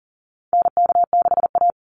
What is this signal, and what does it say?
Morse code, heard as a keyed tone, about 37 wpm, 700 Hz. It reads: NK6A
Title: NA SOTA Chaser Callsigns 40wpm